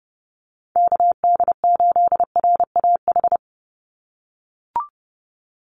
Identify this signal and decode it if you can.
Morse code, heard as a keyed tone, about 30 wpm, 700 Hz. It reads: KD8RAH
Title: USA Call Signs 30wpm